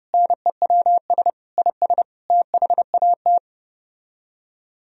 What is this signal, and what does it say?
Morse code, heard as a keyed tone, about 32 wpm, 700 Hz. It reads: NEWS IS THAT